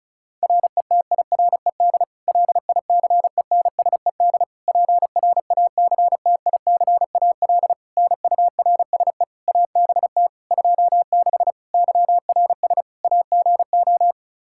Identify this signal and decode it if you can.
Morse code, heard as a keyed tone, about 34 wpm, 700 Hz. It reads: RETIRED LICENSED PRACTICAL NURSE ABT 26 YRS AGO
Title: Practice QSO - Exchange 3 - More Info - 35wpm